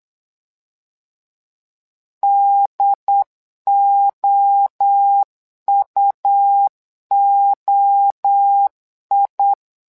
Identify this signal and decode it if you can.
Morse code, heard as a keyed tone, about 8 wpm, 800 Hz. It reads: DOUOI
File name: LEZIONI / 3 MWRO / 3 MWRO 8wpm 2606